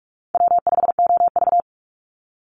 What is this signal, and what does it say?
Morse code, heard as a keyed tone, about 44 wpm, 700 Hz. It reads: W5OV
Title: Easy Call Signs - 45wpm